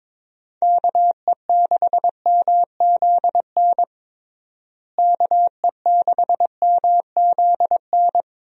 Morse code, heard as a keyed tone, about 22 wpm, 700 Hz. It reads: KE6MZN KE6MZN
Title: USA Call Signs - 6 Characters - Repeated 3x - 22wpm